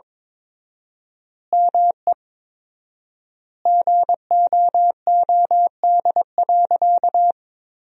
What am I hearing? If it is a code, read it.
Morse code, heard as a keyed tone, about 22 wpm, 700 Hz. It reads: ME GOOD.
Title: 4x Word-Spacing - Sentences from Top 200 Words - 22wpm